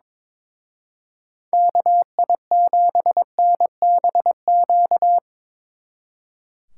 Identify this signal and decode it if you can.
Morse code, heard as a keyed tone, about 24 wpm, 700 Hz. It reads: KI7NBQ
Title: Speed Racing - USA 2x3 Call Signs 22wpm